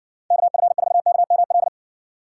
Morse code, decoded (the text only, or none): KF4KML